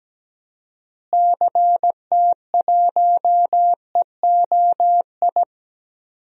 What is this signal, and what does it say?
Morse code, heard as a keyed tone, about 17 wpm, 700 Hz. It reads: CT1EOI
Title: DXCC-100 Call Signs - 6 Characters - Repeated 3x - 17wpm